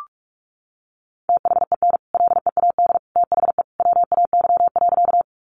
Morse code, heard as a keyed tone, about 44 wpm, 700 Hz. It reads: THEN LEAD THE WAY.
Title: Sentences from Top 500 Words 45wpm